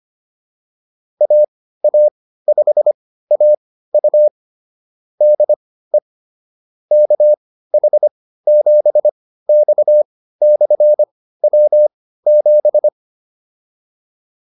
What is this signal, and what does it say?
Morse code, heard as a keyed tone, about 25 wpm, spaced as for 15 wpm, 600 Hz. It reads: AA5AU DE KH7X/W7